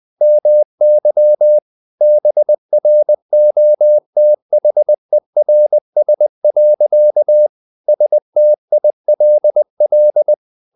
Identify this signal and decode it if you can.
Morse code, heard as a keyed tone, about 20 wpm, 600 Hz. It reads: MY BROTHERS. STILL
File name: the old man and the sea part 2 20wpm